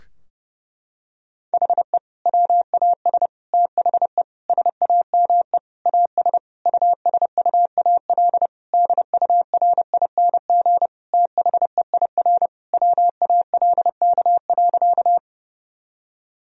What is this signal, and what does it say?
Morse code, heard as a keyed tone, about 30 wpm, 700 Hz. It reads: HE WAS THE SAME AS USUAL DURING THEIR WALK.
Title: Sentences from Top 1000 Words - Review Part 2 - 30wpm